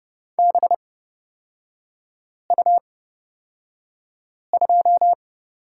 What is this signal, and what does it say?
Morse code, heard as a keyed tone, about 30 wpm, spaced as for 5 wpm, 700 Hz. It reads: BU2
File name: Lesson 36e - BT - Three Characters 30wpm